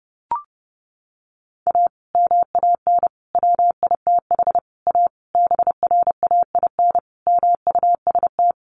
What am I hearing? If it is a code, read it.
Morse code, heard as a keyed tone, about 30 wpm, 700 Hz. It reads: A MAN WITH A BRAIN MUST
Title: Sentences from Top 1100 Words 30wpm